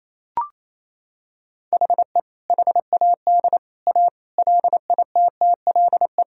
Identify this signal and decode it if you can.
Morse code, heard as a keyed tone, about 28 wpm, 700 Hz. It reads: HE HAD A LITTLE
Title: Sentences from Top 1000 Words - Review Part 1 - 28wpm